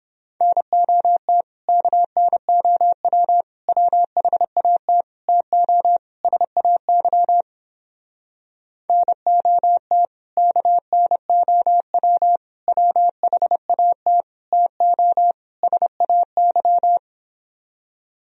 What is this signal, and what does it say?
Morse code, heard as a keyed tone, about 29 wpm, 700 Hz. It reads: NOT KNOW WHAT TO SAY NOT KNOW WHAT TO SAY